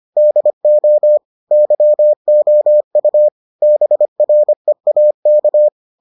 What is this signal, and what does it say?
Morse code, heard as a keyed tone, about 25 wpm, 600 Hz. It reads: DO YOU BREAK